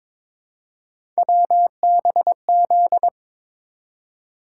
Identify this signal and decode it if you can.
Morse code, heard as a keyed tone, about 22 wpm, 700 Hz. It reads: WBZ
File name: practice / 22wpm / Three Letter-Number Combinations - 22wpm